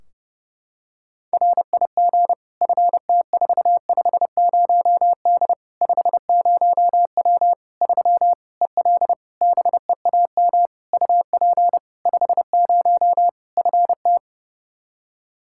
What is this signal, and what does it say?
Morse code, heard as a keyed tone, about 30 wpm, 700 Hz. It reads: RIG FT450D 50W 3 EL BEAM UP 50 FT
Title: K1USN SST QSO Practice 30wpm